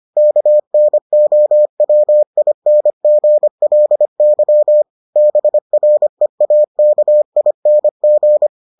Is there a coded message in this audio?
Yes: Morse code from 0.2 to 8.5 s, heard as a keyed tone, about 25 wpm, 600 Hz. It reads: KNOWINGLY BREAKING